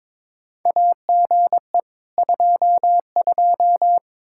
Morse code, heard as a keyed tone, about 22 wpm, 700 Hz. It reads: AGE 22
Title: Practice QSO - Exchange 3 - More Info - 22wpm